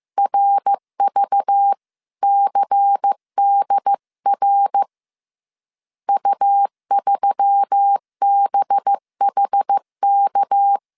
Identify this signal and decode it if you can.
Morse code, heard as a keyed tone, about 15 wpm, 800 Hz. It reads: RVCDR U3BHK
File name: CW / DET / 1 6 1500C S B 14wpm 2958